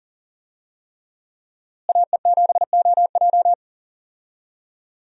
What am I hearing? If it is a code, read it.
Morse code, heard as a keyed tone, about 40 wpm, 700 Hz. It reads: AE7OJ
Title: USA Call Signs - 5 Characters - Repeated 3x - 40wpm